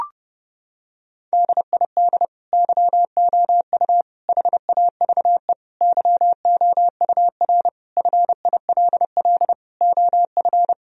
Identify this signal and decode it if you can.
Morse code, heard as a keyed tone, about 30 wpm, 700 Hz. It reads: DID YOU HAVE YOUR FILL OF
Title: Sentences from Top 1000 Words - Review Part 3 - 30wpm